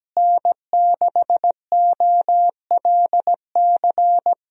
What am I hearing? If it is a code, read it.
Morse code, heard as a keyed tone, about 17 wpm, 700 Hz. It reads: N6OLC
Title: USA Call Signs - 5 Characters - Repeated 2x - 17wpm